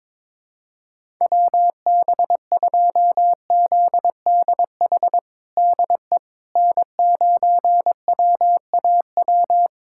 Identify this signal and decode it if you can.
Morse code, heard as a keyed tone, about 22 wpm, 700 Hz. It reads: WB2ZDH DE N9WAW
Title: USA Callsign DE Callsign 22wpm